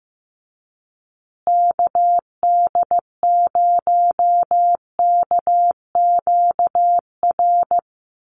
Morse code, heard as a keyed tone, about 15 wpm, 700 Hz. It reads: KD0KQR
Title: USA Call Signs Repeated 2x 15wpm